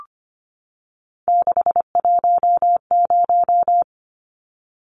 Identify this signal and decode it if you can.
Morse code, heard as a keyed tone, about 25 wpm, 700 Hz. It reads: 610
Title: Numbers - 3-Digit 25wpm